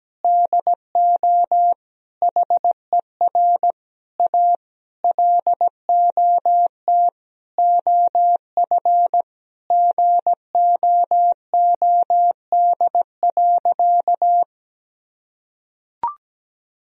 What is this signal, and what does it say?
Morse code, heard as a keyed tone, about 17 wpm, 700 Hz. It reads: DO HER A LOT OF GOOD.
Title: Sentences from Top 1000 Words - Review Part 3 - 17wpm